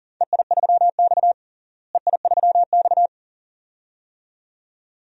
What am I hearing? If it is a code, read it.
Morse code, heard as a keyed tone, about 38 wpm, 700 Hz. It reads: EI3X EI3X
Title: DXCC-100 Call Signs - 4 Characters - Repeated 2x - 40wpm